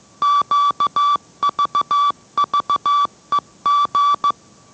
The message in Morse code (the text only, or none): QVVEG